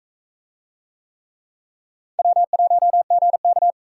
Morse code, heard as a keyed tone, about 42 wpm, 700 Hz. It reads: W1GK